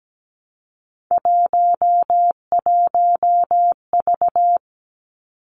Morse code, heard as a keyed tone, about 17 wpm, 700 Hz. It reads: 11V
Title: Three Challenging Letter-Number Combinations 17wpm